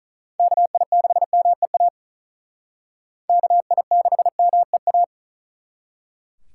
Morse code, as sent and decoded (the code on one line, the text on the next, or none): -.- .. -.... -- . .- / -.- .. -.... -- . .-
KI6MEA KI6MEA